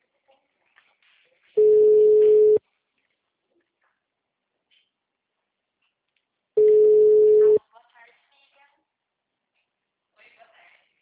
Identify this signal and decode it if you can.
Ringback tone (Europe, 425 Hz), with speech.